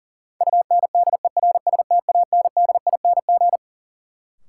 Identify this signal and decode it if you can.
Morse code, heard as a keyed tone, about 40 wpm, 700 Hz. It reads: UNDERSTANDING
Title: Sentences from Top 1000 Words - Review Part 3 - New Word Drill - 40wpm